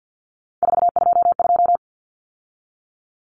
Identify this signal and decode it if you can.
Morse code, heard as a keyed tone, about 50 wpm, 700 Hz. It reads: 422
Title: Numbers - 3-Digit 50wpm